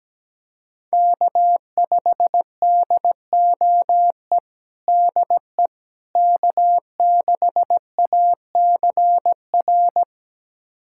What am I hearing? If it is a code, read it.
Morse code, heard as a keyed tone, about 17 wpm, 700 Hz. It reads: K5DOE DE K6ACR
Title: USA Callsign DE Callsign - 5 Characters - 17wpm